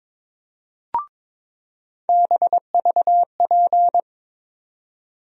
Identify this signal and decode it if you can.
Morse code, heard as a keyed tone, about 22 wpm, 700 Hz. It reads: BVP
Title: USA Call Sign Postfixes - 3 Letters 22wpm